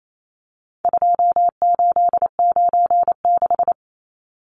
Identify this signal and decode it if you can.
Morse code, heard as a keyed tone, about 28 wpm, 700 Hz. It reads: 2896
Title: Numbers - 4-Digit 28wpm